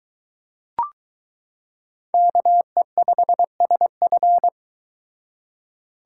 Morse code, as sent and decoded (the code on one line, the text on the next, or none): -.- . ..... ... ..-.
KE5SF